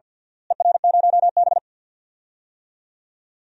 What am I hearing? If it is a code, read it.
Morse code, heard as a keyed tone, about 47 wpm, 700 Hz. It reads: ER0B
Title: DXCC-100 Call Signs - 4 Characters - Repeated 2x - 50wpm